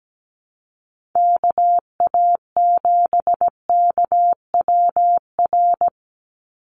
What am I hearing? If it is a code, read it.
Morse code, heard as a keyed tone, about 17 wpm, 700 Hz. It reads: KA7KWR
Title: USA Call Signs 17wpm